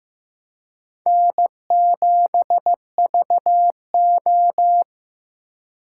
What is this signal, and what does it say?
Morse code, heard as a keyed tone, about 18 wpm, 700 Hz. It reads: N7VO
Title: Speed Racing - USA 1x2 Call Signs 15wpm